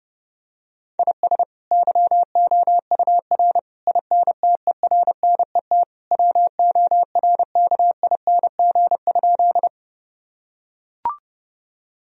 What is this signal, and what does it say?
Morse code, heard as a keyed tone, about 30 wpm, 700 Hz. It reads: IS YOUR INTERNET WORKING?
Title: Sentences from Top 1000 Words - Review Part 4 - 30wpm